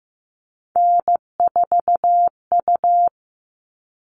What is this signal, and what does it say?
Morse code, heard as a keyed tone, about 15 wpm, 700 Hz. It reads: N4U
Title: All Cut Numbers - 3-Digit 15wpm